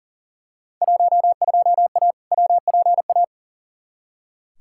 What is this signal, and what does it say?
Morse code, heard as a keyed tone, about 37 wpm, 700 Hz. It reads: 12A WPA
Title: Field Day Exchanges Encore - 40wpm